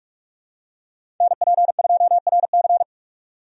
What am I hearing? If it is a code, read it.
Morse code, heard as a keyed tone, about 46 wpm, 700 Hz. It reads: NP2RC
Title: USA Call Signs - 5 Characters - 45wpm